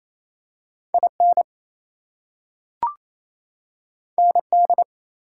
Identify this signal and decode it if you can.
Morse code, heard as a keyed tone, about 28 wpm, 700 Hz. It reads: IN ND
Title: US States and Canadian Provinces 28wpm